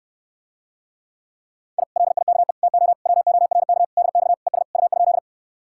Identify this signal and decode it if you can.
Morse code, heard as a keyed tone, about 67 wpm, 700 Hz. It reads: I LIKE TO WORK DX ES U?